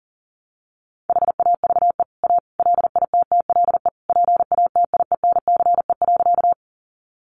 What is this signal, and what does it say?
Morse code, heard as a keyed tone, about 40 wpm, 700 Hz. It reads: HAVE A LITTLE PATIENCE.
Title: Sentences from Top 1300 Words 40wpm